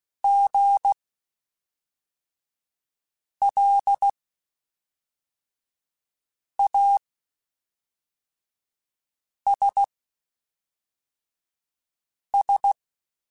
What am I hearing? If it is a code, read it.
Morse code, heard as a keyed tone, about 16 wpm, spaced as for 3 wpm, 800 Hz. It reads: GLASS
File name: Spaced / Chuckles Spaced-mp3s / HalfFull sp @5